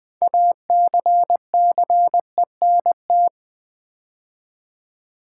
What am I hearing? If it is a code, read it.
Morse code, heard as a keyed tone, about 20 wpm, 700 Hz. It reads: ACCENT
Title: 6 Letter Words - First 3 Letters Given - 20wpm